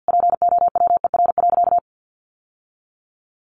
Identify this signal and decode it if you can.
Morse code, heard as a keyed tone, about 50 wpm, 700 Hz. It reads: POWER.
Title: Sentences from Top 600 Words 50wpm